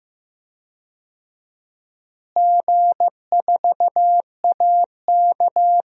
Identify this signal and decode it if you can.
Morse code, heard as a keyed tone, about 15 wpm, 700 Hz. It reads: G4AK
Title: DXCC-100 Call Signs - 4 Characters - Repeated 2x - 15wpm